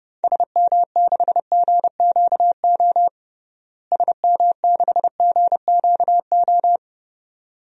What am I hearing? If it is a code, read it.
Morse code, heard as a keyed tone, about 32 wpm, 700 Hz. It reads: SM6GQO SM6GQO
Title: DXCC-100 Call Signs - 6 Characters - Repeated 2x - 30wpm